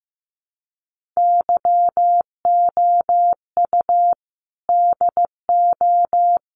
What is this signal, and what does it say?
Morse code, heard as a keyed tone, about 15 wpm, 700 Hz. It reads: YOU DO
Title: Sets of 5 Words 15wpm